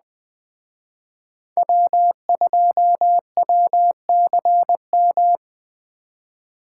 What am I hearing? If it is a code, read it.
Morse code, heard as a keyed tone, about 20 wpm, 700 Hz. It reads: W2WCM
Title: Speed Racing - USA 1x3 Call Signs 17wpm